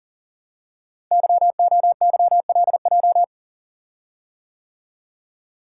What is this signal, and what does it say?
Morse code, heard as a keyed tone, about 40 wpm, 700 Hz. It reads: YOYLJ
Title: Code Groups - 5 Letter:Numbers - 40wpm